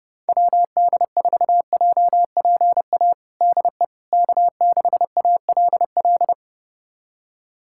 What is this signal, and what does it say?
Morse code, heard as a keyed tone, about 30 wpm, 700 Hz. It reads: WD4JPA DE K6ALL
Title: USA Callsign DE Callsign 30wpm